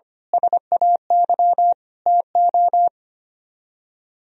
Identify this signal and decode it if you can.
Morse code, heard as a keyed tone, about 30 wpm, 700 Hz. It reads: SAY TO
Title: Speed-Racing - Sets of 5 Words 25wpm